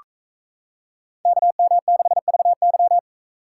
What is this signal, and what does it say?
Morse code, heard as a keyed tone, about 33 wpm, 700 Hz. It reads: KM6VY
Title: Speed Racing - USA Call Signs 28wpm